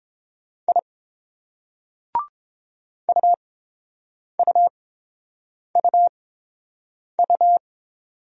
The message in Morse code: I UUUU

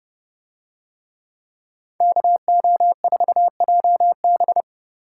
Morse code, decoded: KO4JB

